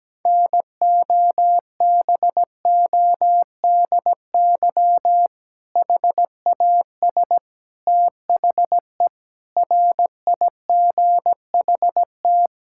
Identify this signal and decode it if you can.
Morse code, heard as a keyed tone, about 17 wpm, 700 Hz. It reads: NOBODY HAS THE RIGHT
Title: Sentences from Top 1600 Words 17wpm